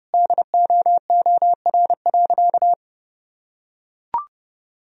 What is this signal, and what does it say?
Morse code, heard as a keyed tone, about 30 wpm, 700 Hz. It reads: DOOR.
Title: Sentences from Top 1000 Words - Review Part 3 - 30wpm